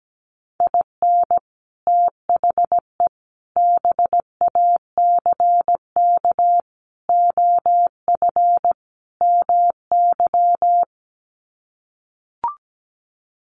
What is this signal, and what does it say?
Morse code, heard as a keyed tone, about 17 wpm, 700 Hz. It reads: IN THE BACK OF MY
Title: Sets of 5 Words Encore 17wpm